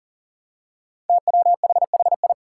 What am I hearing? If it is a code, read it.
Morse code, heard as a keyed tone, about 40 wpm, 700 Hz. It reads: TWHHI
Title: Five Challenging Letter-Number Combinations 40wpm